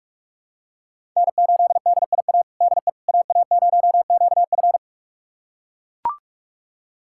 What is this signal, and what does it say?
Morse code, heard as a keyed tone, about 41 wpm, 700 Hz. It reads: N8DIA DE AA0QF
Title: USA Callsign DE Callsign - 5 Characters - 45wpm